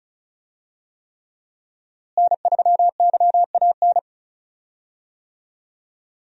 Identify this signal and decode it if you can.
Morse code, heard as a keyed tone, about 36 wpm, 700 Hz. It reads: N3YAN
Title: USA Call Signs - 5 Characters - Repeated 3x - 35wpm